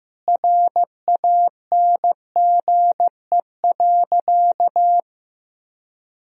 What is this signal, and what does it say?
Morse code, heard as a keyed tone, about 15 wpm, 700 Hz. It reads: RANGE.
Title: Sentences from Top 1000 Words - Review Part 4 - 15wpm